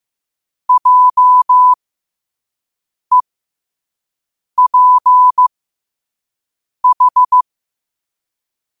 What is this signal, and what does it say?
Morse code, heard as a keyed tone, about 15 wpm, spaced as for 5 wpm, 1000 Hz. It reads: JEPH
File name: qotdfiles / 5wpm